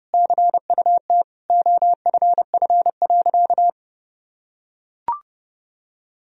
Morse code, heard as a keyed tone, about 30 wpm, 700 Hz. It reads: CUT OFF.
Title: Sentences from Top 1000 Words - Review Part 1 - 30wpm